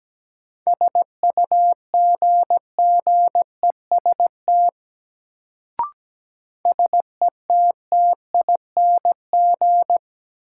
Morse code, heard as a keyed tone, about 17 wpm, 700 Hz. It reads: SUGGEST SETTING
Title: Sentences from Top 1300 Words - New Word Drill Alt - 17wpm